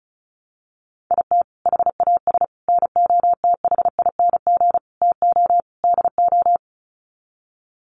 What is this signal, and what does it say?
Morse code, heard as a keyed tone, about 34 wpm, 700 Hz. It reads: IT HAS NOTHING TO DO
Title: Sets of 5 Words Encore 35wpm